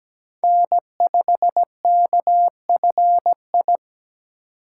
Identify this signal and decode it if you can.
Morse code, heard as a keyed tone, about 20 wpm, 700 Hz. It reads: N5KFI
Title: Speed Racing - USA 1x3 Call Signs 17wpm